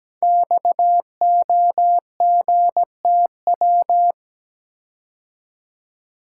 Morse code, heard as a keyed tone, about 17 wpm, 700 Hz. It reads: XOGTW
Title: Code Groups - 5 Letter:Numbers - 17wpm